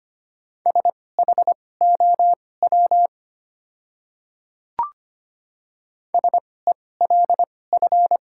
Morse code, heard as a keyed tone, about 25 wpm, spaced as for 18 wpm, 700 Hz. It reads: SHOW SELF
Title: 2x Character Spacing - Top 500 Words - 25wpm